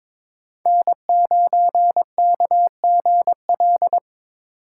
Morse code, heard as a keyed tone, about 22 wpm, 700 Hz. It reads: N9KGL
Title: USA Call Signs - 4x Character Spacing - 22wpm